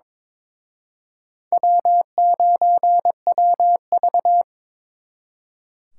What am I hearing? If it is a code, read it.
Morse code, heard as a keyed tone, about 26 wpm, 700 Hz. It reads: W9WV
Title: Speed Racing - USA 1x2 Call Signs 22wpm